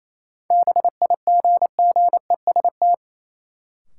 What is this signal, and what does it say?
Morse code, heard as a keyed tone, about 28 wpm, 700 Hz. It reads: BIGGEST
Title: Sets of 4 Words - New Word Drill - 28wpm